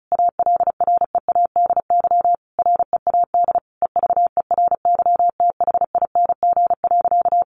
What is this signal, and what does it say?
Morse code, heard as a keyed tone, about 34 wpm, 700 Hz. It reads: ALREADY READ EVERYTHING.
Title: Sentences from Top 700 Words 35wpm